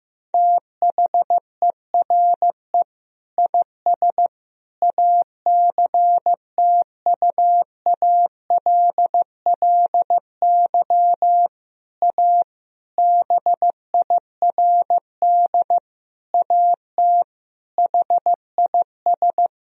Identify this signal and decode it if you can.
Morse code, heard as a keyed tone, about 15 wpm, 700 Hz. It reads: THERE IS ACTUALLY A BIRD AT HIS
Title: Sentences from Top 1000 Words - Review Part 4 - 15wpm